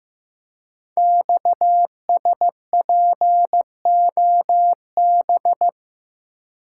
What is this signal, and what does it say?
Morse code, heard as a keyed tone, about 15 wpm, 700 Hz. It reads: XSPOB
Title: Code Groups - 5 Letters - 15wpm